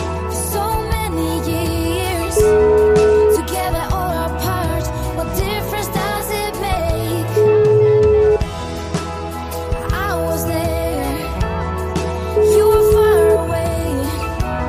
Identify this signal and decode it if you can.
Ringback tone (Europe, 425 Hz), with noise.